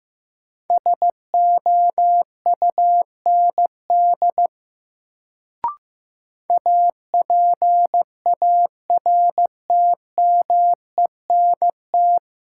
Morse code, heard as a keyed tone, about 15 wpm, 700 Hz. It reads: SOUND APARTMENT
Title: Sentences from Top 900 Words - New Word Drill - 15wpm